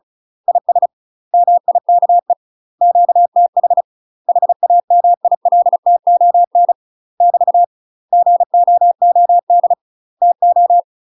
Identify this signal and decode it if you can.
Morse code, heard as a keyed tone, about 34 wpm, 700 Hz. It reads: IS MIKE QTH HAMILTON <BT> GOOD TO
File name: UK MASTER QSOs TRAINING / 35WPM / UK MASTER QSOs 0040